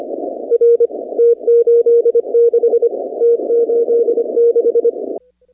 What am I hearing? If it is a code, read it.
Morse code, heard as a keyed tone, about 25 wpm, 500 Hz. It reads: R T86 T86
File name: 2019-cw-wpx